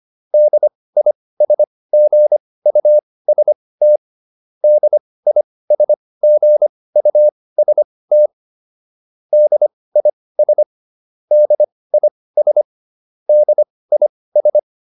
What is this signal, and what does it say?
Morse code, heard as a keyed tone, about 25 wpm, spaced as for 18 wpm, 600 Hz. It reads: DISGUST DISGUST DIS DIS DIS